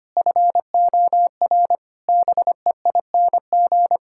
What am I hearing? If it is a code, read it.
Morse code, heard as a keyed tone, about 25 wpm, 700 Hz. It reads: FOR BEING
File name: Sets of 9 Words - 25wpm